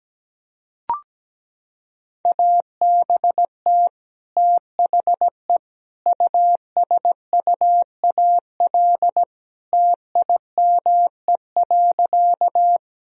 Morse code, heard as a keyed tone, about 17 wpm, 700 Hz. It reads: ABT THE USUAL TIME.
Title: Sentences from Top 400 Words and Abbreviations 17wpm